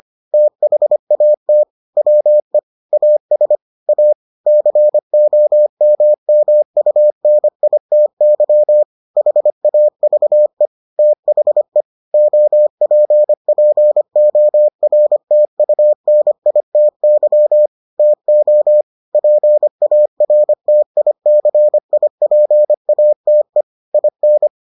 Morse code, heard as a keyed tone, about 25 wpm, 600 Hz. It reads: THAT WE AS A COMMUNITY HAVE THE OPPORTUNITY TO PARTICIPATE IN